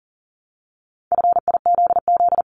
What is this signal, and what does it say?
Morse code, heard as a keyed tone, about 43 wpm, 700 Hz. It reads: FIZZ
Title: Multi-letter Phonemes Repeated 3x 40wpm